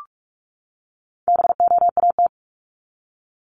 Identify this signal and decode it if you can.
Morse code, heard as a keyed tone, about 44 wpm, 700 Hz. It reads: BOAT
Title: Top 500 Words 45wpm